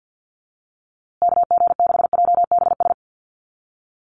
Morse code, heard as a keyed tone, about 50 wpm, 700 Hz. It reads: KG6JBS
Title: USA Call Signs Repeated 3x 50wpm